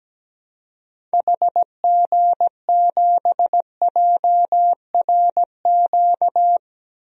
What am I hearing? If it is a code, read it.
Morse code, heard as a keyed tone, about 17 wpm, 700 Hz. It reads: HG7JRQ
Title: DXCC-100 Call Signs - 6 Characters - Repeated 2x - 17wpm